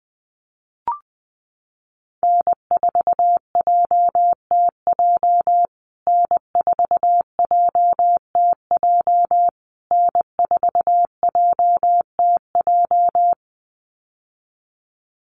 Morse code, heard as a keyed tone, about 20 wpm, 700 Hz. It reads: N4JTJ N4JTJ N4JTJ
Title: USA Call Signs Repeated 3x 20wpm